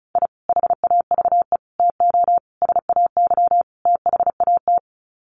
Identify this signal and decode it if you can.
Morse code, heard as a keyed tone, about 34 wpm, 700 Hz. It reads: I HAVE TO SAY THAT
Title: Sets of 5 Words Encore 35wpm